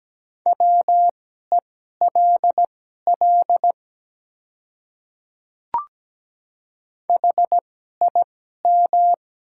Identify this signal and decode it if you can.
Morse code, heard as a keyed tone, about 17 wpm, spaced as for 12 wpm, 700 Hz. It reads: WELL HIM
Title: 2x Character Spacing - Top 200 Words - 17wpm